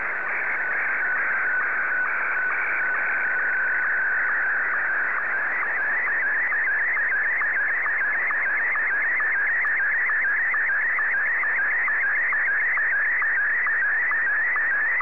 SSTV (slow-scan television), Martin 1.